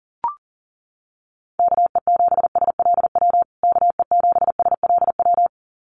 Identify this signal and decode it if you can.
Morse code, heard as a keyed tone, about 40 wpm, 700 Hz. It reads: KE7SLW KE7SLW
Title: USA Call Signs Repeated 2x 40wpm